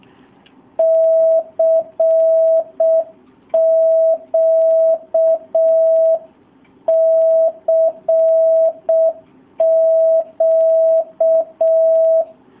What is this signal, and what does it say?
Morse code, heard as a keyed tone, about 6 wpm, 650 Hz. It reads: CQ CQ
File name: jh8cbh / 2019cw / SP07- w